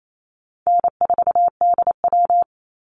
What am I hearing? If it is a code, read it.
Morse code, heard as a keyed tone, about 28 wpm, 700 Hz. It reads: N4DW
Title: Call Signs - 28wpm